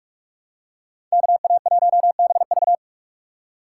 Morse code, heard as a keyed tone, about 43 wpm, 700 Hz. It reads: KA1BV
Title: USA Call Signs - 5 Characters - 2x Character Spacing - 45wpm